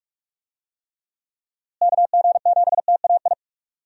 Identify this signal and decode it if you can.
Morse code, heard as a keyed tone, about 42 wpm, 700 Hz. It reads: KG7TAI
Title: USA Call Signs - 6 Characters - Repeated 3x - 45wpm